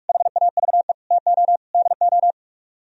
Morse code, heard as a keyed tone, about 49 wpm, 700 Hz. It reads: HAVE TO DO